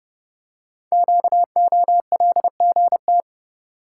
Morse code, heard as a keyed tone, about 30 wpm, 690 Hz. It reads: QOLGT
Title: Code Groups - 5 Letters - 30wpm